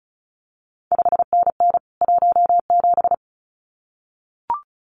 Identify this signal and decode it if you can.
Morse code, heard as a keyed tone, about 36 wpm, 700 Hz. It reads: 5NN 17
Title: CQ WW DX Contest Exchange Encore - 35wpm